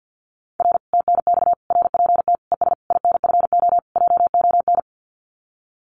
Morse code, heard as a keyed tone, about 49 wpm, 700 Hz. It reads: R TNX RPT ES INFO JON
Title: Common QSO Phrases Encore 50wpm